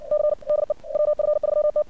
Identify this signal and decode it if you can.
Morse code, heard as a keyed tone, about 63 wpm, 600 Hz. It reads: C DE WF3T
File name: archived / CRSnet